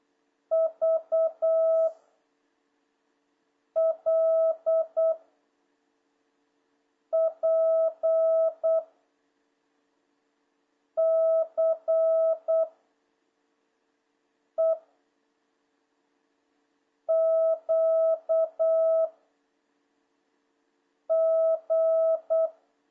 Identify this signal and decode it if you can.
Morse code, heard as a keyed tone, about 8 wpm, 660 Hz. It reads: V L P C E Q G
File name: jh8cbh / 2026cw